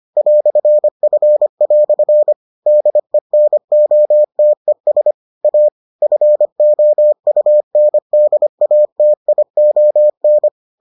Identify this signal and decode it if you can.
Morse code, heard as a keyed tone, about 25 wpm, 600 Hz. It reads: "F" DENOTES A FOUNDATION